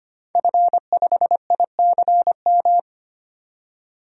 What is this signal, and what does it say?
Morse code, heard as a keyed tone, about 25 wpm, 700 Hz. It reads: F5ICM
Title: DXCC-100 Call Signs - 5 Characters - 25wpm